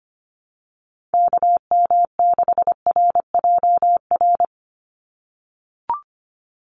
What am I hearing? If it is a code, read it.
Morse code, heard as a keyed tone, about 25 wpm, 700 Hz. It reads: KM6RJR
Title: USA Call Signs 25wpm